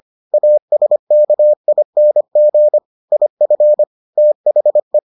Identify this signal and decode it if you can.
Morse code, heard as a keyed tone, about 25 wpm, 600 Hz. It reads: ASKING IF THE